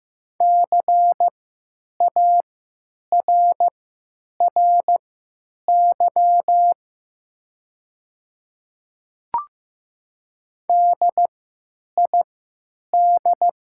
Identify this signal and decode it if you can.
Morse code, heard as a keyed tone, about 15 wpm, spaced as for 9 wpm, 700 Hz. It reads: CARRY DID
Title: 3x Character Spacing - Top 500 Words - 15wpm